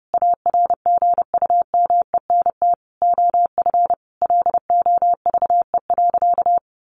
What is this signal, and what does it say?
Morse code, heard as a keyed tone, about 30 wpm, 700 Hz. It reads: ARGUMENT OF LOVE.
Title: Sentences from Top 1400 Words 30wpm